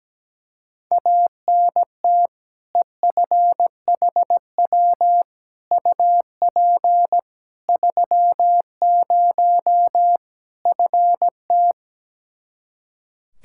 Morse code, heard as a keyed tone, about 17 wpm, 700 Hz. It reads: ANT EFHW UP 30 FT
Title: Practice QSO - Exchange 2 - Info - 17wpm